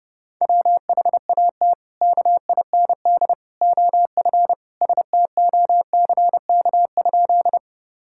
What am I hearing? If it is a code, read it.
Morse code, heard as a keyed tone, about 32 wpm, 700 Hz. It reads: WHAT KIND OF STOCK?
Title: Sentences from Top 1000 Words - Review Part 5 - 30wpm